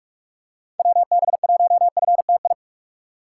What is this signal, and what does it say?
Morse code, heard as a keyed tone, about 44 wpm, 700 Hz. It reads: WB1FTI
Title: USA Call Signs - 6 Characters - 6x Character Spacing - 45wpm